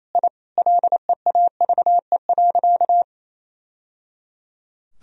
Morse code, heard as a keyed tone, about 27 wpm, 700 Hz. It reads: I LEAVE.
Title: Sentences from Top 1000 Words - Review Part 1 - 28wpm